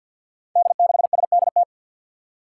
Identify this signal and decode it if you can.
Morse code, heard as a keyed tone, about 50 wpm, 700 Hz. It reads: D6SDT
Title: Five Challenging Letter-Number Combinations 50wpm